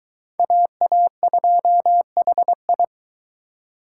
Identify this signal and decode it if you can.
Morse code, heard as a keyed tone, about 23 wpm, 700 Hz. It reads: AA2HI